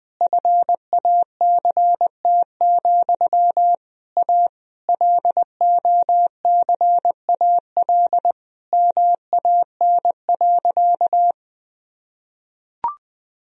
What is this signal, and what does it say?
Morse code, heard as a keyed tone, about 20 wpm, 700 Hz. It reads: FACT, A LOCAL MAN.
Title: Sentences from Top 1000 Words - Review Part 3 - 20wpm